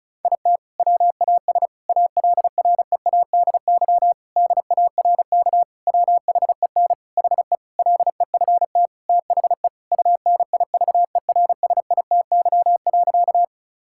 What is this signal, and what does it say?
Morse code, heard as a keyed tone, about 34 wpm, 700 Hz. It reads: IT WAS ALREADY DARK WHEN HE LEFT THE UNIVERSITY.
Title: Sentences from Top 1000 Words - Review Part 5 - 35wpm